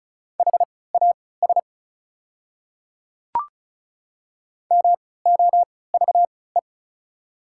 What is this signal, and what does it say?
Morse code, heard as a keyed tone, about 36 wpm, spaced as for 20 wpm, 700 Hz. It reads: HAS MOVE
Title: 3x Character Spacing - Top 200 Words - 35wpm